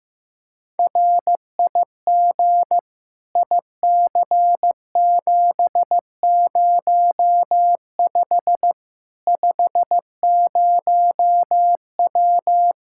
Morse code, heard as a keyed tone, about 16 wpm, 700 Hz. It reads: RIG IC705 50W
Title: Practice QSO - Exchange 2 - Info - 15wpm